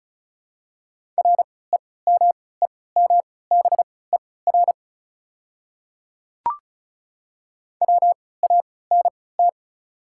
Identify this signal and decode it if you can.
Morse code, heard as a keyed tone, about 34 wpm, spaced as for 20 wpm, 700 Hz. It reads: REMEMBER WANT
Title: 3x Character Spacing - Top 500 Words - 35wpm